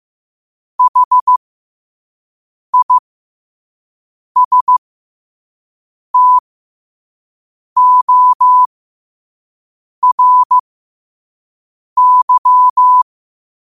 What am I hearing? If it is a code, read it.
Morse code, heard as a keyed tone, about 15 wpm, spaced as for 5 wpm, 1000 Hz. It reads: HISTORY
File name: qotdfiles / 5wpm